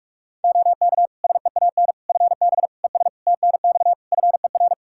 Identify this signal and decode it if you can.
Morse code, heard as a keyed tone, about 46 wpm, 700 Hz. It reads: OK SEAN FB ES TNX FER
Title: Practice QSO - Exchange 2 - Info - 45wpm